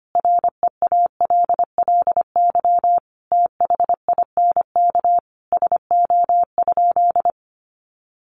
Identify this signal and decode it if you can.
Morse code, heard as a keyed tone, about 25 wpm, 700 Hz. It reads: REALLY THINK SO?
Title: Sentences from Top 600 Words 25wpm